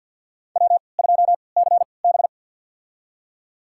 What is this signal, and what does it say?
Morse code, heard as a keyed tone, about 51 wpm, spaced as for 28 wpm, 700 Hz. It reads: W2CB
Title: USA Call Signs - 3x Character Spacing - 50wpm